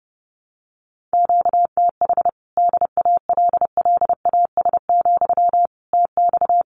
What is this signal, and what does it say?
Morse code, heard as a keyed tone, about 30 wpm, 700 Hz. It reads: QTH DALLAS, TX
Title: Common QSO Phrases Encore 30wpm